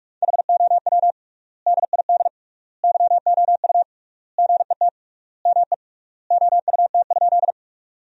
Morse code, heard as a keyed tone, about 42 wpm, 700 Hz. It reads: HOW DID YOU GET ME OUT?
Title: 3x Word-Spacing - Sentences from Top 200 Words - 45wpm